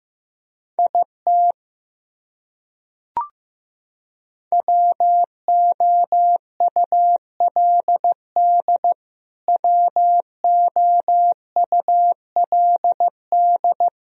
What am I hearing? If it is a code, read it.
Morse code, heard as a keyed tone, about 15 wpm, 700 Hz. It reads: IT WOULD WOULD
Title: Top 100 Words Repeated 15wpm